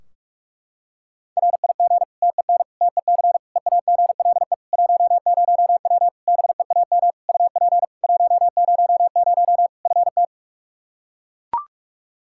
Morse code, heard as a keyed tone, about 41 wpm, 700 Hz. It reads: RIG TEN TEC EAGLE 10W BEAM UP 100 FT
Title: Practice QSO - Exchange 2 - Info - 45wpm